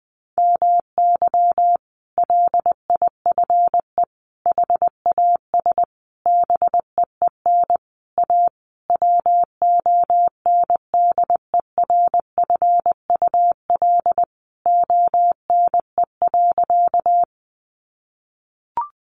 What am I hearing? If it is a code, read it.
Morse code, heard as a keyed tone, about 20 wpm, 700 Hz. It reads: MY LIFE HAS BEEN A WONDERFUL ONE.
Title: Sentences from Top 1100 Words 20wpm